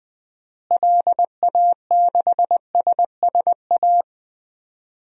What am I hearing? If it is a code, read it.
Morse code, heard as a keyed tone, about 20 wpm, 700 Hz. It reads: LA6SSA
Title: DXCC-100 Call Signs - 6 Characters - Repeated 2x - 20wpm